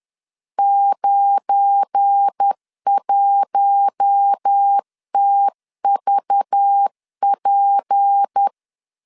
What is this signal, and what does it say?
Morse code, heard as a keyed tone, about 10 wpm, 800 Hz. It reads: 91TVP
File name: LEZIONI / 8 YJ45 / 8 YJ4 10wpm 2633